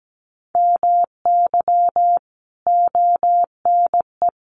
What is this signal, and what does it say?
Morse code, heard as a keyed tone, about 17 wpm, 700 Hz. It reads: MY ONE
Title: Sentences from Top 1200 Words 17wpm